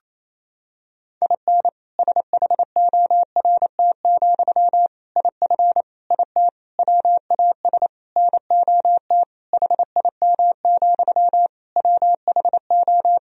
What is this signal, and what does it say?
Morse code, heard as a keyed tone, about 28 wpm, 700 Hz. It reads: IN SHORT, IF IT WAS NOT HIM, WHO
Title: Sentences from Top 1000 Words - Review Part 2 - 28wpm